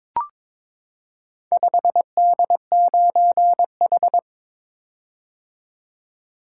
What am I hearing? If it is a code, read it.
Morse code, heard as a keyed tone, about 22 wpm, 700 Hz. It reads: 5D9H
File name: practice / 22wpm / Four Letter-Number Combinations - 22wpm